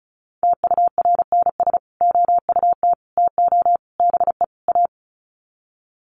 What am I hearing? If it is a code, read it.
Morse code, heard as a keyed tone, about 34 wpm, 700 Hz. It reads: TURNS OUT TO BE A
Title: Sets of 5 Words Encore 35wpm